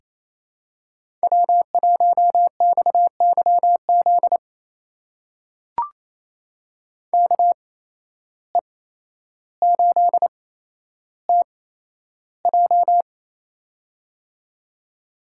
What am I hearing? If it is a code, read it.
Morse code, heard as a keyed tone, about 28 wpm, spaced as for 12 wpm, 700 Hz. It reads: W1XYZ KE8TJ
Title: USA Call Signs - 5 Characters - 8x Character Spacing - 28wpm